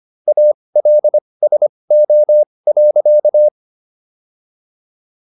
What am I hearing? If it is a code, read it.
Morse code, heard as a keyed tone, about 25 wpm, spaced as for 20 wpm, 600 Hz. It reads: ALSO.